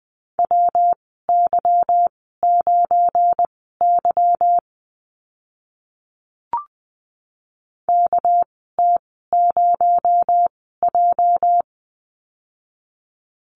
Morse code, heard as a keyed tone, about 20 wpm, spaced as for 14 wpm, 700 Hz. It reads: WY9Y KT0J
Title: 231 Lesson 37g - Letter x - Call Signs 20wpm